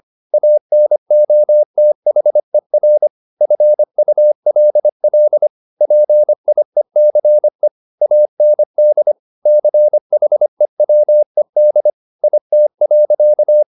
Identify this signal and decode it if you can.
Morse code, heard as a keyed tone, about 25 wpm, 600 Hz. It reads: ANOTHER FULL PIECE AND CHEWED IT.
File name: the old man and the sea part 1 25wpm